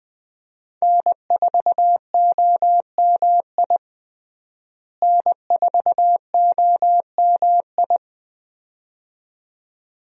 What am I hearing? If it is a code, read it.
Morse code, heard as a keyed tone, about 20 wpm, 700 Hz. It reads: N4OMI N4OMI
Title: USA Call Signs - 5 Characters - Repeated 2x - 20wpm